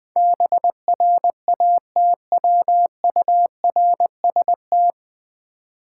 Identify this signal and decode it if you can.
Morse code, heard as a keyed tone, about 20 wpm, 700 Hz. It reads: BRATWURST
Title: Difficult Words with Hints - 20wpm